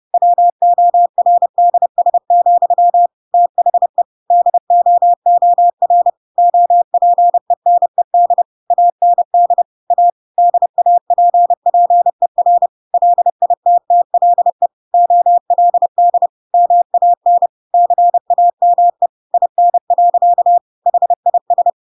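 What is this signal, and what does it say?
Morse code, heard as a keyed tone, about 30 wpm, 700 Hz. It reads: WORDS, THE DOOR OPENED AND A DAPPER LITTLE OLD MAN CAME IN. HIS
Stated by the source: Chapter 2 of The Adventures of Pinocchio read as a book in Morse code (CW) at 30 wpm, by ik1hge.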